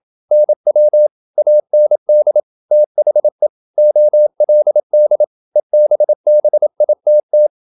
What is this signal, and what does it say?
Morse code, heard as a keyed tone, about 28 wpm, 600 Hz. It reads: NW AND THE OLD EBBITT
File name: Never 0027wpm